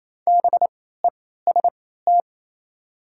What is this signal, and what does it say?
Morse code, heard as a keyed tone, about 28 wpm, spaced as for 16 wpm, 700 Hz. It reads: BEST
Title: 3x Character Spacing - Top 500 Words - 28wpm